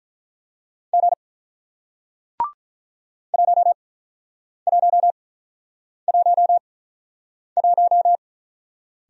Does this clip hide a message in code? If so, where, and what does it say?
Yes: Morse code from 0.9 to 8.2 s, heard as a keyed tone, about 48 wpm, spaced as for 9 wpm, 700 Hz. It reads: G 1111